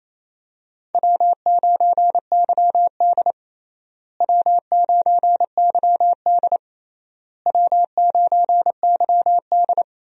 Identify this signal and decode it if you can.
Morse code, heard as a keyed tone, about 28 wpm, 700 Hz. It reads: W9YD W9YD W9YD
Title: USA Call Signs - 4 Characters - Repeated 3x - 28wpm